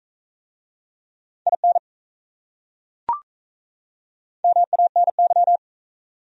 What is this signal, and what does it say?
Morse code, heard as a keyed tone, about 38 wpm, 700 Hz. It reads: IN MANY